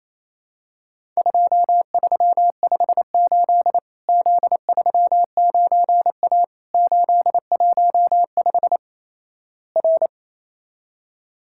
Morse code, heard as a keyed tone, about 28 wpm, 700 Hz. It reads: 2358 Z39A 815 R
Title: Worked All Europe DX Contest - QTC Exchange - 28wpm